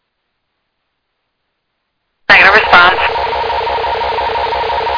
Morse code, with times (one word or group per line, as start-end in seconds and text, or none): none